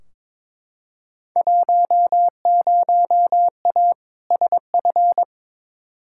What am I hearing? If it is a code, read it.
Morse code, heard as a keyed tone, about 22 wpm, 700 Hz. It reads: 10A SF
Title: Field Day Exchanges Encore - 22wpm